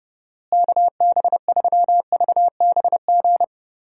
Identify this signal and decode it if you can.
Morse code, heard as a keyed tone, about 30 wpm, 700 Hz. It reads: KB3VBG
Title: USA Call Signs - 6 Characters - Repeated 3x - 30wpm